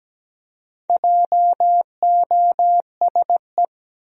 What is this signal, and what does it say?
Morse code, heard as a keyed tone, about 17 wpm, 700 Hz. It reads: JOSE
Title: USA Common Names - Max 4 Letters - 17wpm